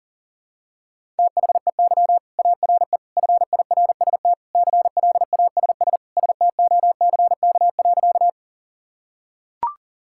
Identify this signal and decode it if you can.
Morse code, heard as a keyed tone, about 40 wpm, 700 Hz. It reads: THEY ARE FIRST CLASS STOCK.
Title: Sentences from Top 1000 Words - Review Part 5 - 40wpm